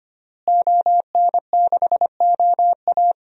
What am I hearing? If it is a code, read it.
Morse code, heard as a keyed tone, about 25 wpm, 700 Hz. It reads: ON6OA
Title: DXCC-100 Call Signs - 5 Characters - Repeated 3x - 25wpm